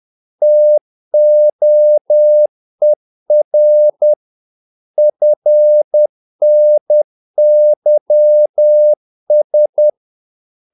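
Morse code, heard as a keyed tone, about 10 wpm, 600 Hz. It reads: TOER FNYS
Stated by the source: Morse practice of short Danish words at 10 wpm, from CW med Gnister.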